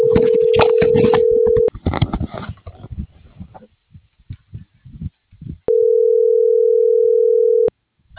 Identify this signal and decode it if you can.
Ringback tone (North America), 440 + 480 Hz.